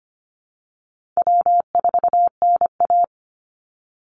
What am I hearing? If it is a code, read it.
Morse code, heard as a keyed tone, about 25 wpm, 700 Hz. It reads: W4NA
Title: Easy Call Signs - 25wpm